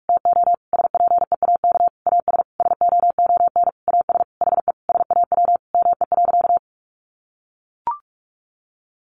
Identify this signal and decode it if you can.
Morse code, heard as a keyed tone, about 44 wpm, 700 Hz. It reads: TO SPEAK AS SOON AS HE SAW ME.
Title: Sentences from Top 1100 Words 45wpm